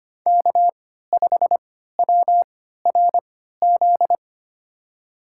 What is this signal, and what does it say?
Morse code, heard as a keyed tone, about 25 wpm, spaced as for 14 wpm, 700 Hz. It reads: K5WRZ
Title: USA Call Signs - 5 Characters - 3x Character Spacing - 25wpm